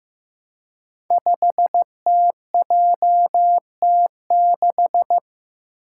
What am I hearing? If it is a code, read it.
Morse code, heard as a keyed tone, about 15 wpm, 700 Hz. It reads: 5TJT6
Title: Five Challenging Letter-Number Combinations 15wpm